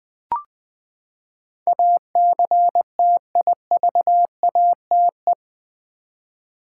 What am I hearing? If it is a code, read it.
Morse code, heard as a keyed tone, about 21 wpm, 700 Hz. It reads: ACTIVATE
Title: Eight Letter Words - 20wpm